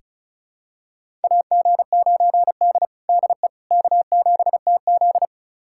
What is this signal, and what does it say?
Morse code, heard as a keyed tone, about 36 wpm, 700 Hz. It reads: AG9D DE K7TZ
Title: USA Callsign DE Callsign - 4 Characters - 35wpm